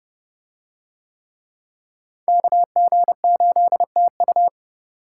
Morse code, heard as a keyed tone, about 30 wpm, 700 Hz. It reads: KG8TU